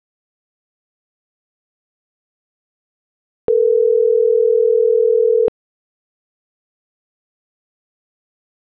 Ringback tone (North America), 440 + 480 Hz.